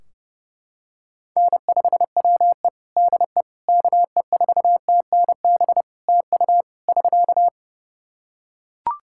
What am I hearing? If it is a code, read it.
Morse code, heard as a keyed tone, about 32 wpm, 700 Hz. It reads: N5WE DE KE4TNB TU <SK>
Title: K1USN SST QSO Practice 30wpm